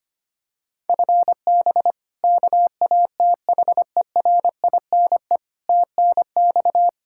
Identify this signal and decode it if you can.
Morse code, heard as a keyed tone, about 25 wpm, 700 Hz. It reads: FB KATHERINE TNX
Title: Practice QSO - Exchange 3 - More Info - 25wpm